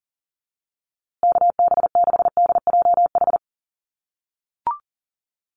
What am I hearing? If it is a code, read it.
Morse code, heard as a keyed tone, about 40 wpm, 700 Hz. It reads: KB6DJH
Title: USA Call Signs Repeated 3x 40wpm